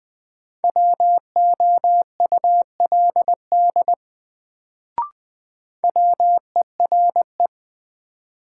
Morse code, heard as a keyed tone, about 21 wpm, 700 Hz. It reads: WOULD WERE
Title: Sets of 2 Words - New Word Drill - 20wpm